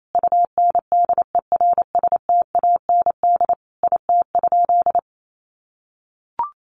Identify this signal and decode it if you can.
Morse code, heard as a keyed tone, about 28 wpm, 700 Hz. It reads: UNDERSTAND IT?
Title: Sentences from Top 600 Words 28wpm